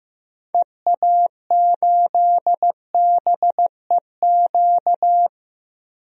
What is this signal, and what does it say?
Morse code, heard as a keyed tone, about 15 wpm, 700 Hz. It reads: EA8BEQ
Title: DXCC-100 Call Signs - 6 Characters - Repeated 2x - 15wpm